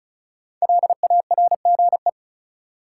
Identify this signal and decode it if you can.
Morse code, heard as a keyed tone, about 36 wpm, 700 Hz. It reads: LARGE
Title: Sets of 4 Words - New Word Drill - 35wpm